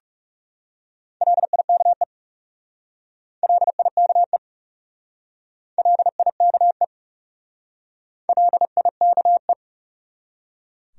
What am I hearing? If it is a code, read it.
Morse code, heard as a keyed tone, about 38 wpm, 700 Hz. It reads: LIKE LIKE LIKE LIKE